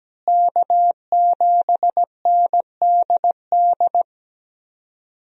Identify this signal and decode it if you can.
Morse code, heard as a keyed tone, about 17 wpm, 700 Hz. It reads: K7NDD
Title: USA Call Signs - 5 Characters - Repeated 3x - 17wpm